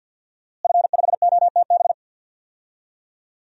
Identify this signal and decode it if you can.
Morse code, heard as a keyed tone, about 46 wpm, 700 Hz. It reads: F5OTB
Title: DXCC-100 Call Signs - 5 Characters - Repeated 3x - 50wpm